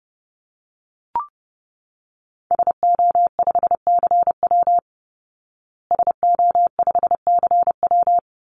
Morse code, heard as a keyed tone, about 30 wpm, 700 Hz. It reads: SO5CW SO5CW
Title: Call Signs Repeated 2x - 30wpm